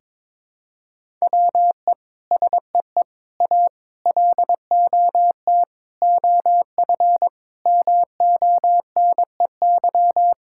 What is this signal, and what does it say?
Morse code, heard as a keyed tone, about 21 wpm, 700 Hz. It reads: WE SEE A LOT OF MONEY
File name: Sets of 9 Words - 22wpm